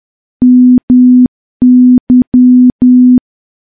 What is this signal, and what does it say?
Morse code, heard as a keyed tone, about 10 wpm, 250 Hz. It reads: MY
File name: morse / morse-training-10WPM